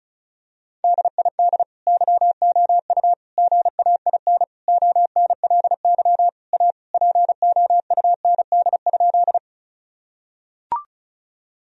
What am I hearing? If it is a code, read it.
Morse code, heard as a keyed tone, about 34 wpm, 700 Hz. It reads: DID YOU GAIN ONLY A POUND?
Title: Sentences from Top 1000 Words - Review Part 2 - 35wpm